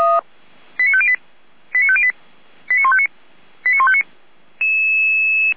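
Five-tone selective call (ZVEI), heard as quick runs of five tones, tones of about 70 ms. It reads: 89508 89508 89159 89159
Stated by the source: ZVEI five-tone call received by SDR.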